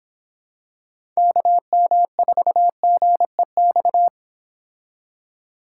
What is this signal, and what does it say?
Morse code, heard as a keyed tone, about 25 wpm, 700 Hz. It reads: KM4GEX